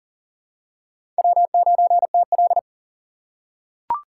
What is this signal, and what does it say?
Morse code, heard as a keyed tone, about 40 wpm, 700 Hz. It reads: W9TL
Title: USA Call Signs - 4 Characters - 4x Character Spacing - 40wpm